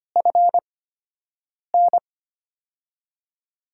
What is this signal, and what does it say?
Morse code, heard as a keyed tone, about 25 wpm, spaced as for 7 wpm, 700 Hz. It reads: FN
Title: USA Call Signs - 6 Characters - 8x Character Spacing - 25wpm